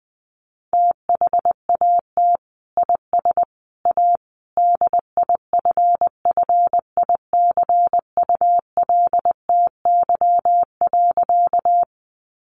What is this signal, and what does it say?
Morse code, heard as a keyed tone, about 20 wpm, 700 Hz. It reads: THAT IS A DIFFICULTY.
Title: Sentences from Top 1500 Words 20wpm